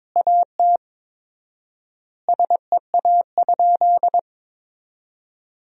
Morse code, heard as a keyed tone, about 21 wpm, 700 Hz. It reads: AT SEA?
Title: 4x Word-Spacing - Sentences from Top 300 Words - 22wpm